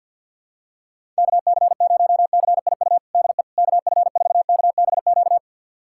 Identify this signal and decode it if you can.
Morse code, heard as a keyed tone, about 52 wpm, 700 Hz. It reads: KC0CIU DE KF4KBQ
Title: USA Callsign DE Callsign - 6 Characters - 50wpm